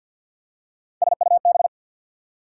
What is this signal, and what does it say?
Morse code, heard as a keyed tone, about 52 wpm, 690 Hz. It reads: SUB
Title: Prefixes and Suffixes 50wpm